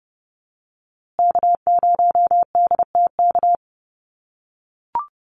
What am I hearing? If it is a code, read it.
Morse code, heard as a keyed tone, about 30 wpm, 700 Hz. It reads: K0DTK
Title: USA Call Signs Repeated 3x 30wpm